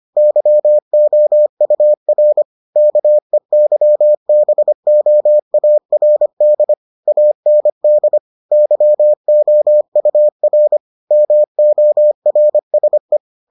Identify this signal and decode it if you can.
Morse code, heard as a keyed tone, about 25 wpm, 600 Hz. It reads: YOUR KEYBOARD AND YOUR MORSE